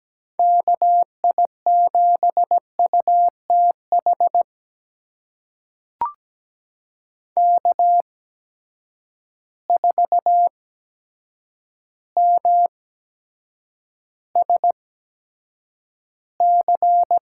Morse code, heard as a keyed tone, about 17 wpm, spaced as for 8 wpm, 700 Hz. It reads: KI7UTH K4MSC
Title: USA Call Signs - 8x Character Spacing - 17wpm